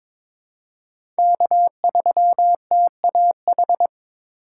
Morse code, heard as a keyed tone, about 22 wpm, 700 Hz. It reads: K3TAH
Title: USA Call Signs - 2x Character Spacing - 22wpm